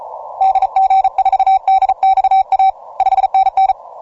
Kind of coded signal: Morse code